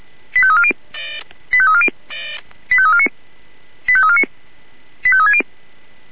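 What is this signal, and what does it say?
Five-tone selective call (ZVEI), heard as quick runs of five tones, tones of about 70 ms. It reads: 85340 85340 85349 85349 85360
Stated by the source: ZVEI five-tone selective call, received by SDR.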